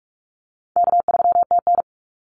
Morse code, heard as a keyed tone, about 46 wpm, 700 Hz. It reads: K3TN
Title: Easy Call Signs - 45wpm